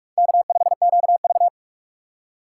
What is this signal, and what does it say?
Morse code, heard as a keyed tone, about 46 wpm, 700 Hz. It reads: K5QV
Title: USA Call Signs - 4 Characters - 3x Character Spacing - 45wpm